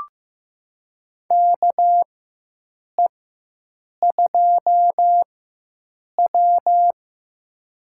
Morse code, heard as a keyed tone, about 15 wpm, spaced as for 7 wpm, 700 Hz. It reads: KE2W
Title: USA Call Signs - 4 Characters - 4x Character Spacing - 15wpm